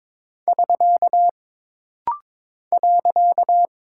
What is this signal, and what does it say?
Morse code, heard as a keyed tone, about 22 wpm, 700 Hz. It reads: <SK> .
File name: practice / 22wpm / Single Punctuation-Prosign - Rapid-Fire - 22wpm